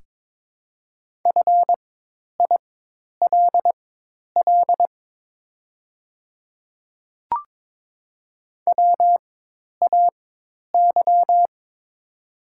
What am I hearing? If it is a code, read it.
Morse code, heard as a keyed tone, about 23 wpm, spaced as for 10 wpm, 700 Hz. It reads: FILL WAY
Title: 4x Character Spacing - Top 500 Words - 22wpm